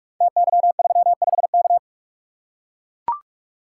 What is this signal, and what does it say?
Morse code, heard as a keyed tone, about 44 wpm, 700 Hz. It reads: TY35K
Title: Code Groups - 5 Letter:Numbers - 45wpm